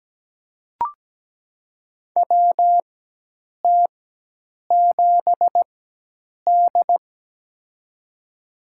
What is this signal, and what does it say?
Morse code, heard as a keyed tone, about 17 wpm, spaced as for 8 wpm, 700 Hz. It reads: WT7D
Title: USA Call Signs - 4x Character Spacing - 17wpm